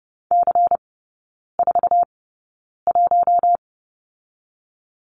Morse code, heard as a keyed tone, about 30 wpm, spaced as for 9 wpm, 690 Hz. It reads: C41
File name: Lesson 17e - Letter u - Three Characters 30wpm